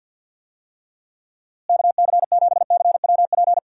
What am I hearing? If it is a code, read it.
Morse code, heard as a keyed tone, about 52 wpm, 700 Hz. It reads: KC7CWP
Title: USA Call Signs - 6 Characters - Repeated 3x - 50wpm